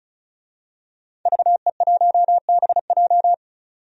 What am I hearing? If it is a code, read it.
Morse code, heard as a keyed tone, about 36 wpm, 700 Hz. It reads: VE1BJ
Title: DXCC-100 Call Signs - 5 Characters - 35wpm